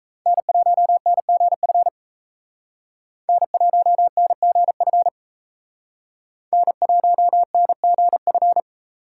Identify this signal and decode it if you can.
Morse code, heard as a keyed tone, about 35 wpm, 700 Hz. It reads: N1NGF N1NGF N1NGF